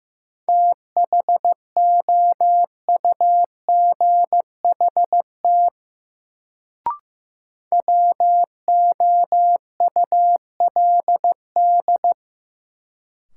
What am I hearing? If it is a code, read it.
Morse code, heard as a keyed tone, about 15 wpm, 700 Hz. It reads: THOUGHT WOULD
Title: Sets of 2 Words - New Word Drill - 15wpm